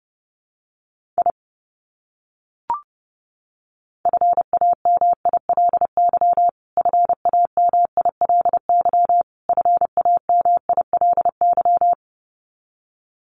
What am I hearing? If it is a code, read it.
Morse code, heard as a keyed tone, about 30 wpm, 700 Hz. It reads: I FAMILY FAMILY FAMILY
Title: Top 500 Words Repeated 30wpm